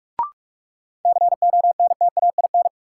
Morse code, heard as a keyed tone, about 46 wpm, 700 Hz. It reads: CONTAIN
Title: Sentences from Top 500 Words - New Word Drill - 45wpm